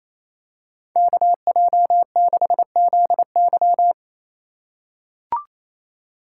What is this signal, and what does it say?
Morse code, heard as a keyed tone, about 28 wpm, 700 Hz. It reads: KJ6ZY
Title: USA Call Signs - 5 Characters - 2x Character Spacing - 28wpm